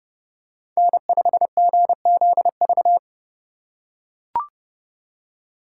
Morse code, heard as a keyed tone, about 30 wpm, 700 Hz. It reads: N5GZV